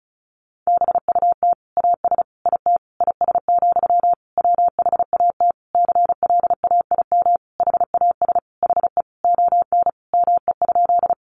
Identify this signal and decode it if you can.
Morse code, heard as a keyed tone, about 34 wpm, 700 Hz. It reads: BUT AS IT IS, WHAT CLAIM HAS HE ON ME?
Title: Sentences from Top 1500 Words 35wpm